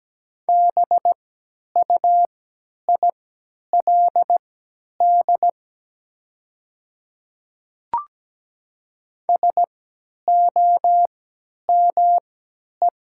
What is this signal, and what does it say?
Morse code, heard as a keyed tone, about 17 wpm, spaced as for 10 wpm, 700 Hz. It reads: BUILD SOME
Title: 3x Character Spacing - Top 200 Words - 17wpm